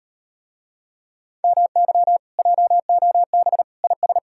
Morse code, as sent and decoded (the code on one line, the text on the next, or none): -- -.-- / .--- --- -... / .. ...
MY JOB IS